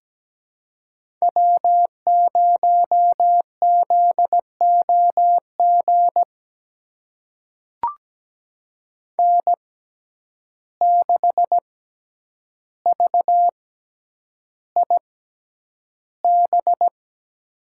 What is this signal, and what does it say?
Morse code, heard as a keyed tone, about 17 wpm, spaced as for 9 wpm, 700 Hz. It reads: W0ZOG N6VIB
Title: USA Call Signs - 5 Characters - 6x Character Spacing - 17wpm